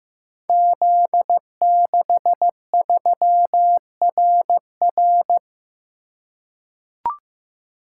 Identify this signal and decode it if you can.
Morse code, heard as a keyed tone, about 15 wpm, 700 Hz. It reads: Z63RR
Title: Code Groups - 5 Letter:Numbers - 15wpm